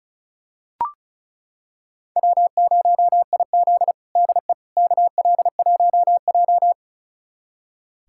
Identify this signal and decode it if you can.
Morse code, heard as a keyed tone, about 36 wpm, 700 Hz. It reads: W0IZ DE KL1J
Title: USA Callsign DE Callsign - 4 Characters - 35wpm